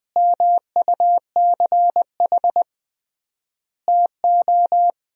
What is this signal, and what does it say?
Morse code, heard as a keyed tone, about 20 wpm, 700 Hz. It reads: MUCH TO
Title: 3x Word-Spacing - Sentences from Top 400 Words - 20wpm